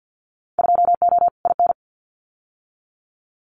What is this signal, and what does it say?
Morse code, heard as a keyed tone, about 50 wpm, 700 Hz. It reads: 2O IN
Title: Winter Field Day Exchanges 50wpm